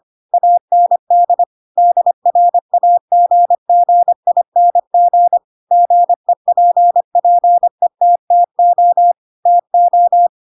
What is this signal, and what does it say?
Morse code, heard as a keyed tone, about 25 wpm, 700 Hz. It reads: AND DRAGGING GEPPETTO TO
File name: Pinocchio / Pinocchio ch3 25 25 ik1hge